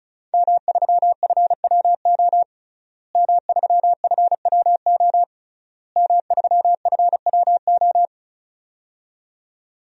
Morse code, heard as a keyed tone, about 34 wpm, 700 Hz. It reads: M3FWO M3FWO M3FWO
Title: DXCC-100 Call Signs - 5 Characters - Repeated 3x - 35wpm